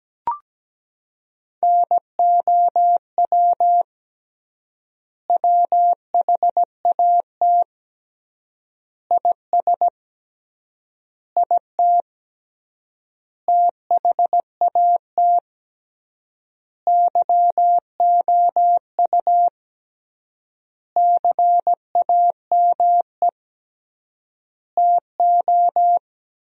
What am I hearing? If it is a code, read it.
Morse code, heard as a keyed tone, about 17 wpm, 700 Hz. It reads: NOW WHAT IS IT THAT YOU CAME TO
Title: 3x Word-Spacing - Sentences from Top 200 Words - 17wpm